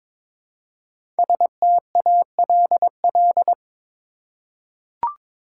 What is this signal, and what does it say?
Morse code, heard as a keyed tone, about 22 wpm, 700 Hz. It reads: STALL
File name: 5 Letter Words - First 3 Letters Given - 22wpm